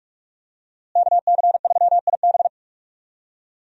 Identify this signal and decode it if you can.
Morse code, heard as a keyed tone, about 46 wpm, 700 Hz. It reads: KC3IB
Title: USA Call Signs - 5 Characters - Repeated 3x - 45wpm